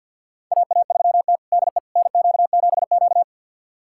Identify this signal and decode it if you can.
Morse code, heard as a keyed tone, about 50 wpm, 700 Hz. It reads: AA3T DE NQ7Q
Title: USA Callsign DE Callsign - 4 Characters - 50wpm